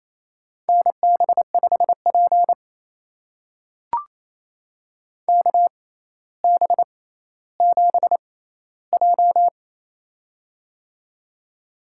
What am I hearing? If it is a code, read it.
Morse code, heard as a keyed tone, about 28 wpm, spaced as for 14 wpm, 700 Hz. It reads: NB5P KB7J
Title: USA Call Signs - 4 Characters - 6x Character Spacing - 28wpm